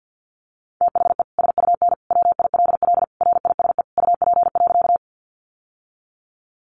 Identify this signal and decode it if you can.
Morse code, heard as a keyed tone, about 50 wpm, 700 Hz. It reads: THE SUN WILL RISE UP.
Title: Sentences from Top 900 Words 50wpm